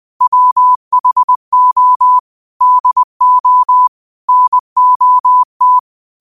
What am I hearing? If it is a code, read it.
Morse code, heard as a keyed tone, about 20 wpm, 1000 Hz. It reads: WHO DO NOT
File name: qotdfiles / 20wpm